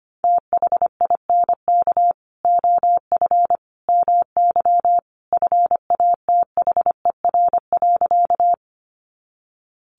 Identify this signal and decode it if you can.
Morse code, heard as a keyed tone, about 25 wpm, 700 Hz. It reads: THINK OF MY FATHER.
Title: Sentences from Top 300 Words 25wpm